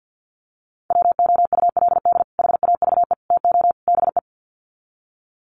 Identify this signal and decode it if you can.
Morse code, heard as a keyed tone, about 48 wpm, 700 Hz. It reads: WOULD HAVE TO BE
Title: Sets of 4 Words 50wpm